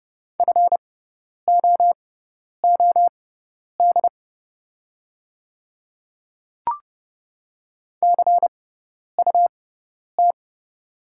Morse code, heard as a keyed tone, about 30 wpm, spaced as for 10 wpm, 700 Hz. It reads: FOOD CUT
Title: 6x Character Spacing - Top 500 Words - 30wpm